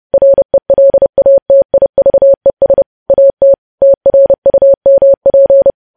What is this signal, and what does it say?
Morse code, heard as a keyed tone, about 30 wpm, 550 Hz. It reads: RELATIVES AT TRUMP